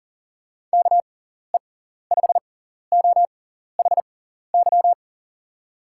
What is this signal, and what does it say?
Morse code, heard as a keyed tone, about 40 wpm, spaced as for 14 wpm, 700 Hz. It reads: KE5OHY
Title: USA Call Signs - 6 Characters - 6x Character Spacing - 40wpm